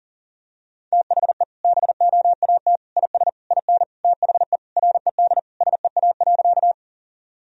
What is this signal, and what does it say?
Morse code, heard as a keyed tone, about 40 wpm, 700 Hz. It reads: THE BOAT IS IN THE RED SEA.
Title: Sentences from Top 1000 Words - Review Part 2 - 40wpm